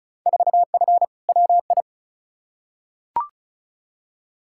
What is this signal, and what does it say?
Morse code, heard as a keyed tone, about 36 wpm, 700 Hz. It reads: 4F WI
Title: Field Day Exchanges Encore - 35wpm